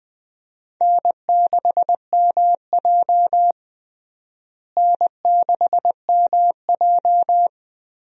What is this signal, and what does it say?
Morse code, heard as a keyed tone, about 20 wpm, 700 Hz. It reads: N6MJ N6MJ
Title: USA Call Signs - 4 Characters - Repeated 3x - 20wpm